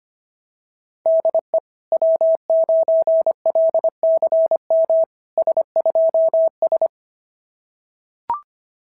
Morse code, heard as a keyed tone, about 24 wpm, 650 Hz. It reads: DE W9LCM S2S
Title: SOTA QSO Practice 25wpm